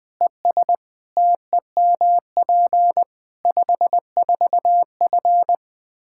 Morse code, heard as a keyed tone, about 20 wpm, 700 Hz. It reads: ES TEMP 54F
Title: Practice QSO - Exchange 2 - Info - 20wpm